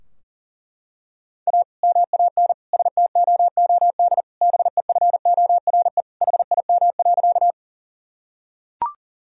Morse code, heard as a keyed tone, about 40 wpm, 700 Hz. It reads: A MAN STOOD BEFORE HIM.
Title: Sentences from Top 1000 Words - Review Part 2 - 40wpm